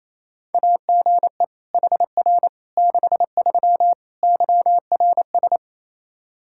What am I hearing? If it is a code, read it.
Morse code, heard as a keyed tone, about 27 wpm, 700 Hz. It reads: AGE HR 63 YRS
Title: K1USN SST QSO Practice 28wpm